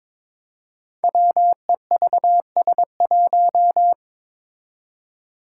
Morse code, heard as a keyed tone, about 22 wpm, 700 Hz. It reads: WEVS1
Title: Five Challenging Letter-Number Combinations 22wpm